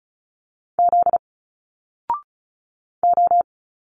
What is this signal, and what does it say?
Morse code, heard as a keyed tone, about 36 wpm, 700 Hz. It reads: Z O
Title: Single Letters 35wpm